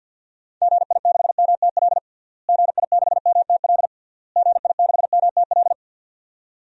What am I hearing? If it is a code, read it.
Morse code, heard as a keyed tone, about 53 wpm, 700 Hz. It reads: GI6MTL GI6MTL GI6MTL
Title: DXCC-100 Call Signs - 6 Characters - Repeated 3x - 50wpm